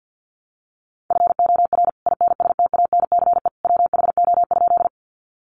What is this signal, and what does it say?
Morse code, heard as a keyed tone, about 50 wpm, 700 Hz. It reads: FOR INSTANCE WHO?
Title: Sentences from Top 800 Words 50wpm